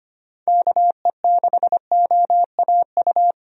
Morse code, heard as a keyed tone, about 25 wpm, 700 Hz. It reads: KE6OAU
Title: USA Call Signs - 6 Characters - Repeated 3x - 25wpm